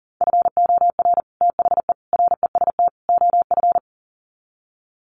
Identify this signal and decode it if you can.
Morse code, heard as a keyed tone, about 40 wpm, 700 Hz. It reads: FOR THE REST OF
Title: Sets of 4 Words 40wpm